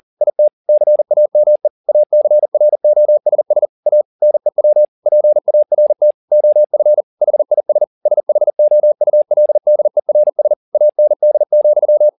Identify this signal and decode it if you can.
Morse code, heard as a keyed tone, about 40 wpm, 600 Hz. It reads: IT CAME ACROSS A NEW PART OF HIS SHOULDERS AND,